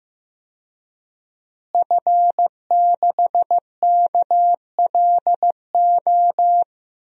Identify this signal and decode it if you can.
Morse code, heard as a keyed tone, about 15 wpm, 700 Hz. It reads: F6KLO
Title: DXCC-100 Call Signs - 5 Characters - Repeated 3x - 15wpm